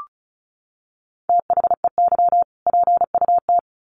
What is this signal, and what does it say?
Morse code, heard as a keyed tone, about 34 wpm, 700 Hz. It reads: THEY PUT
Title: Sentences from Top 1500 Words 35wpm